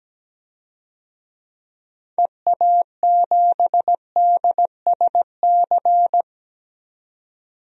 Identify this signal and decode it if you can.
Morse code, heard as a keyed tone, about 17 wpm, 700 Hz. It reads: EA7DSC
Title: DXCC-100 Call Signs - 6 Characters - 17wpm